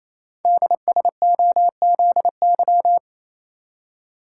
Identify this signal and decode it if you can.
Morse code, heard as a keyed tone, about 27 wpm, 700 Hz. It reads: DSOZY
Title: Code Groups - 5 Letters - 28wpm